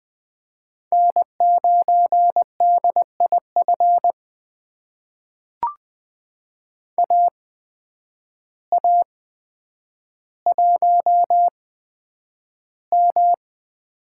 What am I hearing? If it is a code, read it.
Morse code, heard as a keyed tone, about 20 wpm, spaced as for 9 wpm, 700 Hz. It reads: N9DIF AA1M
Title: USA Call Signs - 5 Characters - 8x Character Spacing - 20wpm